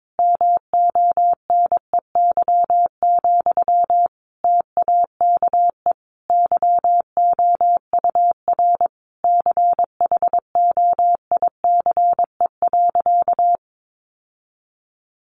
Morse code, heard as a keyed tone, about 22 wpm, 700 Hz. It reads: MONEY, TAKE YOUR CHOICE.
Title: Sentences from Top 900 Words 22wpm